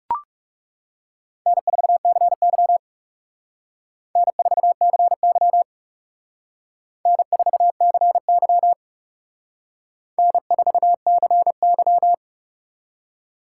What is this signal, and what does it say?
Morse code, heard as a keyed tone, about 36 wpm, 700 Hz. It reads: N4CY N4CY N4CY N4CY
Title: Speed Racing - USA 1x2 Call Signs 30wpm